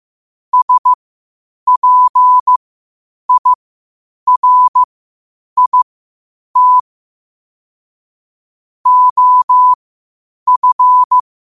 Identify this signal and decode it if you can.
Morse code, heard as a keyed tone, about 15 wpm, spaced as for 8 wpm, 1000 Hz. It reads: SPIRIT OF
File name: qotdfiles / 8wpm